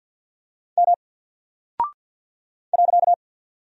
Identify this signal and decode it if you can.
Morse code, heard as a keyed tone, about 48 wpm, 700 Hz. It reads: M .
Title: Single Character (Letter:Number:Punctuation:Prosign) 50wpm